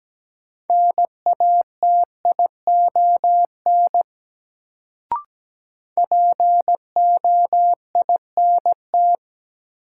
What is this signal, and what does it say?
Morse code, heard as a keyed tone, about 17 wpm, 700 Hz. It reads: NATION POINT
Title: Sets of 3 Words - New Word Drill - 17wpm